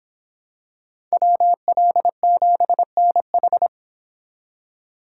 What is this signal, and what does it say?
Morse code, heard as a keyed tone, about 25 wpm, 700 Hz. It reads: WL7NH